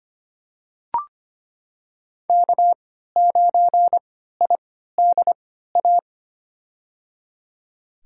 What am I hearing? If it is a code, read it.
Morse code, heard as a keyed tone, about 25 wpm, spaced as for 14 wpm, 700 Hz. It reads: K9IDA
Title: USA Call Signs - 5 Characters - 3x Character Spacing - 25wpm